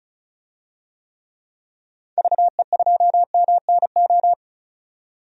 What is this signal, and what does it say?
Morse code, heard as a keyed tone, about 36 wpm, 700 Hz. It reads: VE2MNO
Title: DXCC-100 Call Signs - 6 Characters - 35wpm